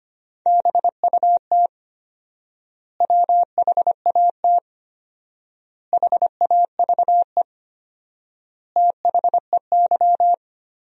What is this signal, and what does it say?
Morse code, heard as a keyed tone, about 25 wpm, 700 Hz. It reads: BUT WHAT HAVE THEY
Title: 4x Word-Spacing - Sentences from Top 100 Words - 25wpm